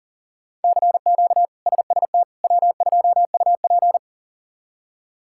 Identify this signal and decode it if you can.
Morse code, heard as a keyed tone, about 37 wpm, 700 Hz. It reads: CQ SST W2UP
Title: K1USN SST QSO Practice - 40wpm